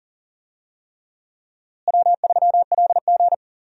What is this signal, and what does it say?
Morse code, heard as a keyed tone, about 40 wpm, 700 Hz. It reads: W3LG
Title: USA Call Signs - 4 Characters - Repeated 3x - 40wpm